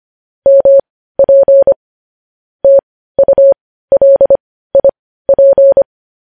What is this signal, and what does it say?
Morse code, heard as a keyed tone, about 25 wpm, spaced as for 15 wpm, 550 Hz. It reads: MP TULIP
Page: news headlines in morse code at 15 wpm